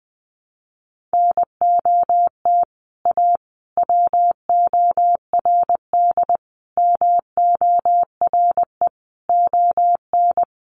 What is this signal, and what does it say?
Morse code, heard as a keyed tone, about 20 wpm, 700 Hz. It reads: NOT A WORD MORE ON
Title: Sentences from Top 800 Words 20wpm